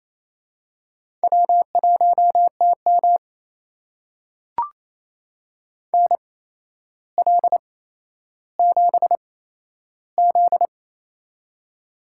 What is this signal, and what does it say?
Morse code, heard as a keyed tone, about 28 wpm, spaced as for 12 wpm, 700 Hz. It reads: W1TM NL7Z
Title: USA Call Signs - 4 Characters - 8x Character Spacing - 28wpm